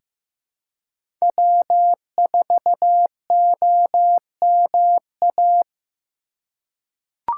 Morse code, heard as a keyed tone, about 15 wpm, 700 Hz. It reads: W4OMA
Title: USA Call Signs - 5 Characters - 3x Character Spacing - 15wpm